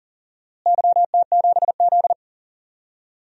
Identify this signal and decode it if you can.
Morse code, heard as a keyed tone, about 37 wpm, 700 Hz. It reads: YT7Z
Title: DXCC-100 Call Signs - 4 Characters - Repeated 2x - 40wpm